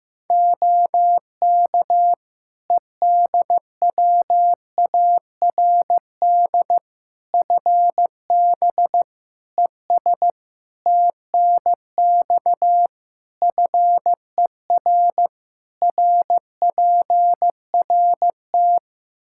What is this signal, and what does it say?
Morse code, heard as a keyed tone, about 15 wpm, 700 Hz. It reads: OK EDWARD FB ES TNX FER RPRT
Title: Practice QSO - Exchange 2 - Info - 15wpm